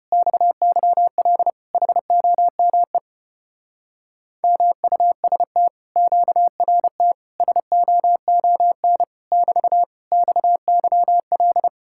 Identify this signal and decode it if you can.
Morse code, heard as a keyed tone, about 33 wpm, 700 Hz. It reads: XYL HOME MUST QRT SOON <BT> XYL